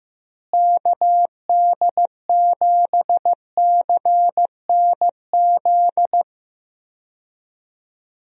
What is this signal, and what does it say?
Morse code, heard as a keyed tone, about 15 wpm, 700 Hz. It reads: KD7CNZ
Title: USA Call Signs - 6 Characters - Repeated 2x - 15wpm